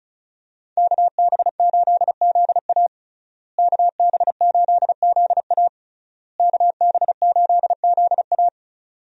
Morse code, heard as a keyed tone, about 36 wpm, 700 Hz. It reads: KB8ZA KB8ZA KB8ZA
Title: USA Call Signs - 5 Characters - Repeated 3x - 35wpm